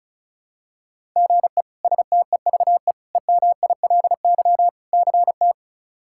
Morse code, heard as a keyed tone, about 36 wpm, 700 Hz. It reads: GE STEVE EMILY CT
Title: K1USN SST QSO Practice - 35wpm